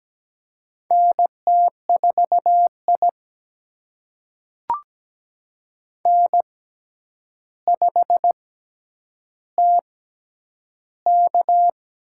Morse code, heard as a keyed tone, about 17 wpm, spaced as for 9 wpm, 700 Hz. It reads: NT4I N5TK
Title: USA Call Signs - 4 Characters - 6x Character Spacing - 17wpm